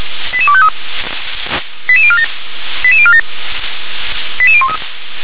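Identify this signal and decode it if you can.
Five-tone selective call (ZVEI), heard as quick runs of five tones, tones of about 70 ms. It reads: 88253 88047 88047 88013
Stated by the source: ZVEI five-tone selective call 88161 received on SDR.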